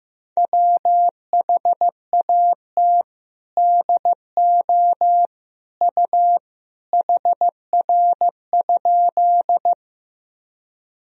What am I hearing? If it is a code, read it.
Morse code, heard as a keyed tone, about 15 wpm, 700 Hz. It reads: WHAT DO U HR?
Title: Sentences from Top 300 Words and Abbreviations 15wpm